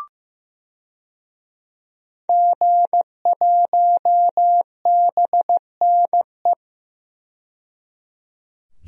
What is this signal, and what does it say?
Morse code, heard as a keyed tone, about 16 wpm, 700 Hz. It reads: G1BNE
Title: DXCC-100 Call Signs - 5 Characters - 15wpm